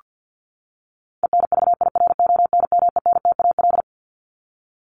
Morse code, heard as a keyed tone, about 48 wpm, 700 Hz. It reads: ENVIRONMENTAL
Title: Top 1001-1250 Words 50wpm